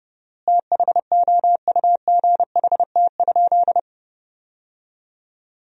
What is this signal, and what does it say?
Morse code, heard as a keyed tone, about 30 wpm, 700 Hz. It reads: THOUGHT?
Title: 4x Word-Spacing - Sentences from Top 300 Words - 30wpm